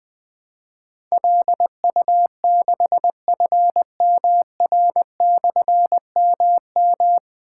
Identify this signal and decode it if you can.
Morse code, heard as a keyed tone, about 21 wpm, 700 Hz. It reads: LU6FMR/MM
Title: Difficult Call Signs 20wpm